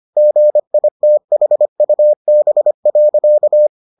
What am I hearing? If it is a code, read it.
Morse code, heard as a keyed tone, about 25 wpm, 600 Hz. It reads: GITHUB.